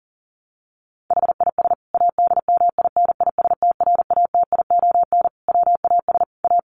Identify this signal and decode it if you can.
Morse code, heard as a keyed tone, about 37 wpm, 700 Hz. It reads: HIS ADMINISTRATION WAS A
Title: Sentences from Top 1300 Words 40wpm